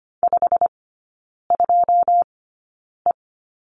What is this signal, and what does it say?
Morse code, heard as a keyed tone, about 25 wpm, spaced as for 9 wpm, 690 Hz. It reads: 52E
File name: Lesson 15e - Number 5 - Three Characters 25wpm